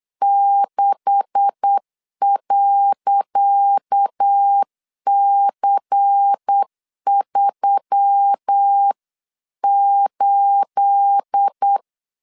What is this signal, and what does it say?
Morse code, heard as a keyed tone, about 8 wpm, 800 Hz. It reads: 6.C38
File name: LEZIONI / 10 SIGNS / 10C DOT 8wpm 3649